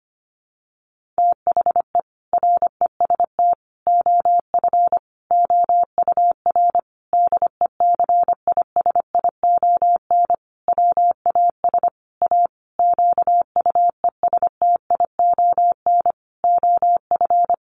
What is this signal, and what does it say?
Morse code, heard as a keyed tone, about 25 wpm, 700 Hz. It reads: THE REST OF OUR DECISION WAS A QUESTION OF
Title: Sentences from Top 1400 Words 25wpm